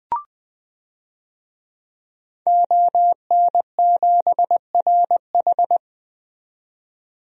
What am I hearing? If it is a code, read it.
Morse code, heard as a keyed tone, about 20 wpm, 700 Hz. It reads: ON7RH
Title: DXCC-100 Call Signs - 5 Characters - 20wpm